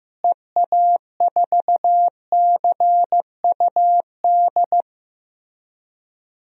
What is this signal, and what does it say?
Morse code, heard as a keyed tone, about 15 wpm, 700 Hz. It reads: EA4CUD
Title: DXCC-100 Call Signs - 6 Characters - Repeated 2x - 15wpm